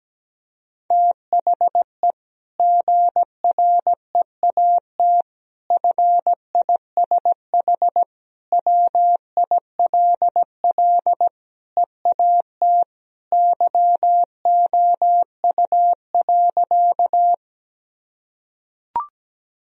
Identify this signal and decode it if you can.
Morse code, heard as a keyed tone, about 17 wpm, 700 Hz. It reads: THE GREAT FISH WILL EAT YOU.
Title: Sentences from Top 1000 Words - Review Part 1 - 17wpm